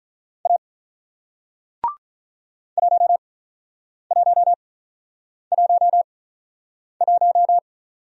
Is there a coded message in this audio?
Yes: Morse code from 0.5 to 7.6 s, heard as a keyed tone, about 48 wpm, spaced as for 9 wpm, 690 Hz. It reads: A 1111